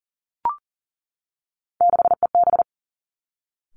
Morse code, heard as a keyed tone, about 41 wpm, 700 Hz. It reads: 6EB
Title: Three Challenging Letter-Number Combinations 40wpm